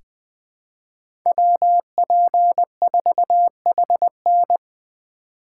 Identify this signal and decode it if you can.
Morse code, heard as a keyed tone, about 20 wpm, 700 Hz. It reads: WP4HN
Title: USA Call Signs - 5 Characters - 2x Character Spacing - 20wpm